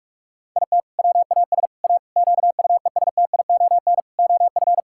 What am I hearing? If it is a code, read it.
Morse code, heard as a keyed tone, about 44 wpm, 700 Hz. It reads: IT WAS A QUESTION OF
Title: Sentences from Top 1000 Words - Review Part 6 - 45wpm